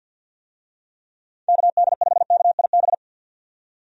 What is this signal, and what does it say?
Morse code, heard as a keyed tone, about 47 wpm, 700 Hz. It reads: KD5KIB
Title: USA Call Signs - 6 Characters - 50wpm